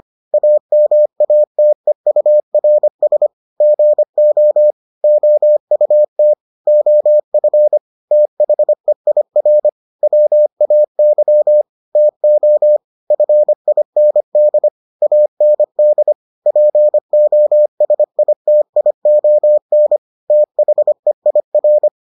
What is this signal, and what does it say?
Morse code, heard as a keyed tone, about 25 wpm, 600 Hz. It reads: AMATEURS GO OUT OF THEIR WAY TO FIND AND POSITION THEIR